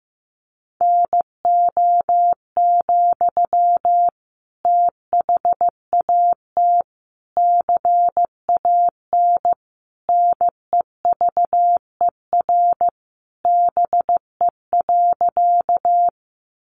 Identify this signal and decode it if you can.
Morse code, heard as a keyed tone, about 15 wpm, 700 Hz. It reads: NO, THAT CAN NEVER BE.
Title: Sentences from Top 300 Words 15wpm